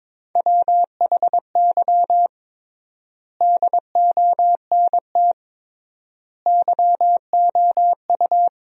Morse code, heard as a keyed tone, about 21 wpm, 700 Hz. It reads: WHY DONT YOU
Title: 3x Word-Spacing - Sentences from Top 300 Words - 22wpm